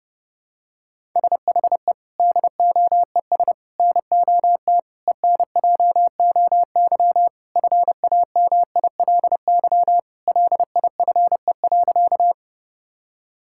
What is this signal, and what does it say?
Morse code, heard as a keyed tone, about 30 wpm, 700 Hz. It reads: SHE DOES NOT ENJOY FAMILY LIFE.
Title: Sentences from Top 1000 Words - Review Part 5 - 30wpm